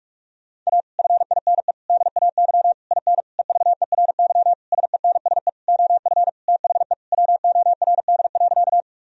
Morse code, heard as a keyed tone, about 46 wpm, 700 Hz. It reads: A FINE DAY IN EVERY SENSE OF THE WORD.
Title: Sentences from Top 1000 Words - Review Part 4 - 45wpm